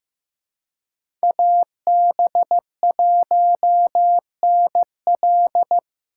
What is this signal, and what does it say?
Morse code, heard as a keyed tone, about 15 wpm, 700 Hz. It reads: AB1NL
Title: USA Call Signs - 2x Character Spacing - 15wpm